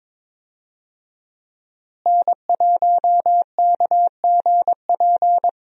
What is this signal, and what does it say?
Morse code, heard as a keyed tone, about 22 wpm, 700 Hz. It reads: N1KGP
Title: USA Call Signs - 5 Characters - Repeated 3x - 22wpm